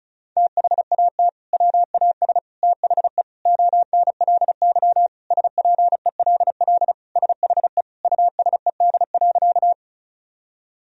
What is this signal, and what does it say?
Morse code, heard as a keyed tone, about 34 wpm, 700 Hz. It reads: THAT WAS THE ONLY SPELL SHE USED.
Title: Sentences from Top 1000 Words - Review Part 1 - 35wpm